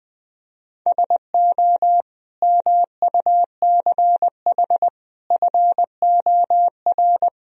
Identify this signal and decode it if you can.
Morse code, heard as a keyed tone, about 21 wpm, 700 Hz. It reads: SO MUCH FOR
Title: Speed-Racing - Sets of 5 Words 20wpm